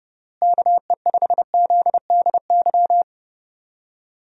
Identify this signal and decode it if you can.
Morse code, heard as a keyed tone, about 32 wpm, 700 Hz. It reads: KE5ZDY
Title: USA Call Signs - 6 Characters - Repeated 3x - 30wpm